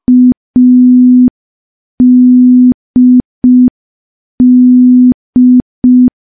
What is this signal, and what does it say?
Morse code, heard as a keyed tone, about 5 wpm, 250 Hz. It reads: ADD
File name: morse / morse-training-05WPM